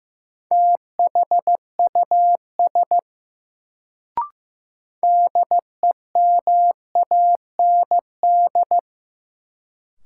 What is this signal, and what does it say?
Morse code, heard as a keyed tone, about 15 wpm, 700 Hz. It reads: THUS DEMAND
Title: Sentences from Top 800 Words - New Word Drill - 15wpm